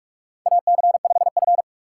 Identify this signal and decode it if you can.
Morse code, heard as a keyed tone, about 41 wpm, 700 Hz. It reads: AC5F
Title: USA Call Signs - 4 Characters - 2x Character Spacing - 45wpm